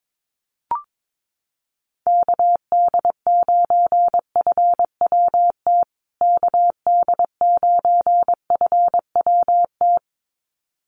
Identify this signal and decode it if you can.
Morse code, heard as a keyed tone, about 22 wpm, 700 Hz. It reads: KD9FWT KD9FWT
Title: USA Call Signs Repeated 2x 22wpm